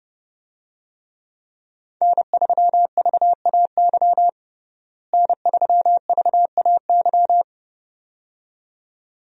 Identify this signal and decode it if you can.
Morse code, heard as a keyed tone, about 30 wpm, 700 Hz. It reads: N3VAY N3VAY
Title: USA Call Signs - 5 Characters - Repeated 2x - 30wpm